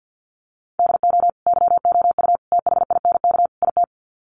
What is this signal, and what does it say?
Morse code, heard as a keyed tone, about 50 wpm, 700 Hz. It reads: DO YOU THINK IT
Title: Sets of 4 Words 50wpm